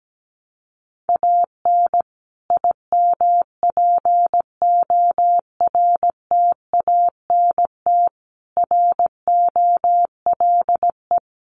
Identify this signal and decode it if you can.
Morse code, heard as a keyed tone, about 17 wpm, 700 Hz. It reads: AN IMPORTANT ROLE
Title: Sets of 5 Words Encore2 17wpm